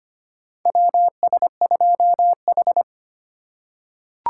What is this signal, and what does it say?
Morse code, heard as a keyed tone, about 25 wpm, 700 Hz. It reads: WS2H
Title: USA Call Signs - 4 Characters - 3x Character Spacing - 25wpm